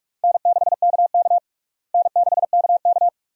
Morse code, heard as a keyed tone, about 42 wpm, 700 Hz. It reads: N6KK N6KK
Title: USA Call Signs - 4 Characters - Repeated 2x - 45wpm